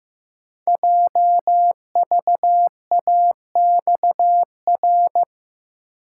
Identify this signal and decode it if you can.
Morse code, heard as a keyed tone, about 15 wpm, 700 Hz. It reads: JVAXR
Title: Code Groups - 5 Letter:Numbers - 15wpm